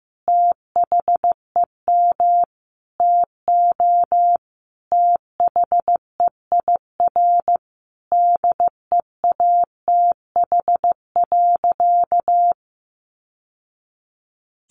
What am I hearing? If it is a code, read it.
Morse code, heard as a keyed tone, about 15 wpm, 700 Hz. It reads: THEM TO THEIR DEATH.
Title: Sentences from Top 1100 Words 15wpm